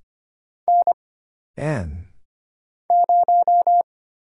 Morse code, heard as a keyed tone, about 25 wpm, 700 Hz. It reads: N 0
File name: Single Character (Letter:Number:Punctuation:Prosign) - Mind-Melt - 25wpm